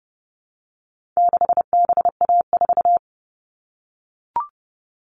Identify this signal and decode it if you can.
Morse code, heard as a keyed tone, about 30 wpm, 700 Hz. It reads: 6BA4
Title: All Cut Numbers - 4-Digit 30wpm